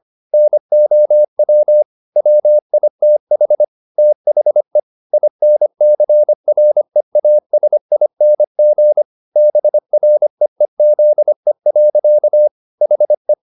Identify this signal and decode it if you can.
Morse code, heard as a keyed tone, about 25 wpm, 600 Hz. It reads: NOW WITH THE INCREASING BREEZE. HE